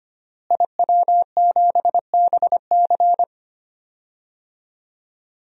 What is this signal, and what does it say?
Morse code, heard as a keyed tone, about 25 wpm, 700 Hz. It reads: IW7BC
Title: DXCC-100 Call Signs - 5 Characters - Repeated 3x - 25wpm